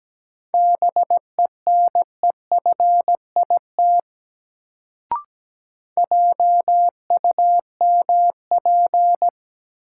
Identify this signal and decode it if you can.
Morse code, heard as a keyed tone, about 17 wpm, 700 Hz. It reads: BENEFIT JUMP
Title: Sentences from Top 1300 Words - New Word Drill Alt - 17wpm